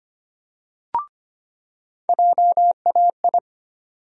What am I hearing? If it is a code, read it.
Morse code, heard as a keyed tone, about 25 wpm, 700 Hz. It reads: JAI
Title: USA Call Sign Postfixes - 3 Letters 25wpm